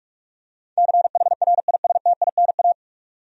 Morse code, heard as a keyed tone, about 46 wpm, 700 Hz. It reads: CHRISTINA
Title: USA Common Names - 45wpm